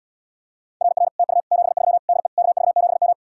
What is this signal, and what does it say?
Morse code, heard as a keyed tone, about 78 wpm, 700 Hz. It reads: RR TU 73 DE KV7M